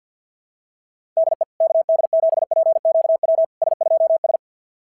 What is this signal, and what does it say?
Morse code, heard as a keyed tone, about 52 wpm, 690 Hz. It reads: DE KD7PQW S2S
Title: SOTA QSO Practice 50wpm